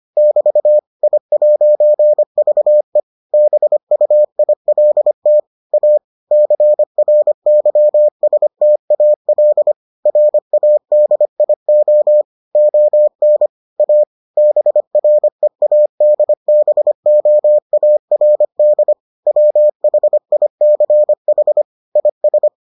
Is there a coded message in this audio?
Yes: Morse code from 0.2 to 22.5 s, heard as a keyed tone, about 25 wpm, 600 Hz. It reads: <BT> I'VE BUILT A CRYSTAL RADIO ON A BREADBOARD WHICH IS